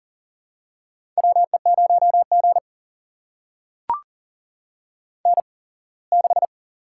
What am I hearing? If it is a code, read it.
Morse code, heard as a keyed tone, about 40 wpm, spaced as for 24 wpm, 700 Hz. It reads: WE0G N6
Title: USA Call Signs - 8x Character Spacing - 40wpm